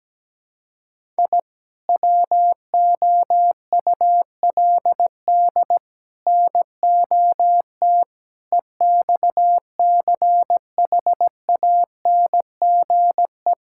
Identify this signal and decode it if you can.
Morse code, heard as a keyed tone, about 17 wpm, 700 Hz. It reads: I WOULD NOT EXCHANGE
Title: Sentences from Top 1000 Words - Review Part 6 - 17wpm